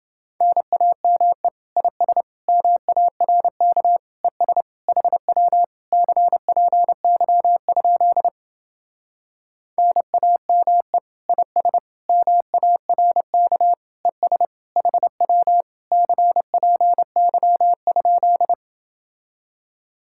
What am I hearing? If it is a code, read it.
Morse code, heard as a keyed tone, about 29 wpm, 700 Hz. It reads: NAME IS MARK ES HW CPY? NAME IS MARK ES HW CPY?